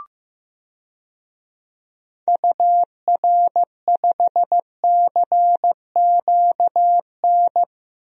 Morse code, heard as a keyed tone, about 16 wpm, 700 Hz. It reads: UR5CQN
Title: DXCC-100 Call Signs - 6 Characters - Repeated 3x - 15wpm